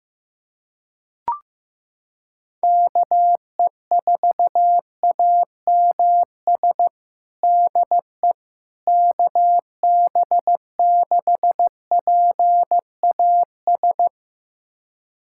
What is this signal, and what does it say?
Morse code, heard as a keyed tone, about 15 wpm, 700 Hz. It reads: KE4AMS DE KB6PAS
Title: USA Callsign DE Callsign 15wpm